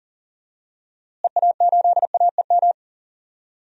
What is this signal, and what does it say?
Morse code, heard as a keyed tone, about 40 wpm, 700 Hz. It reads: EA8AEM
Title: DXCC-100 Call Signs - 6 Characters - Repeated 3x - 40wpm